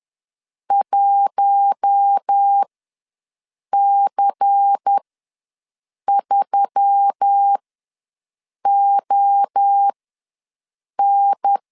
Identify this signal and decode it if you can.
Morse code, heard as a keyed tone, about 10 wpm, spaced as for 6 wpm, 800 Hz. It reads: 1C3ON
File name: LEZIONI / 6 PKX0 / 6 PKX0 ws 10wpm 4220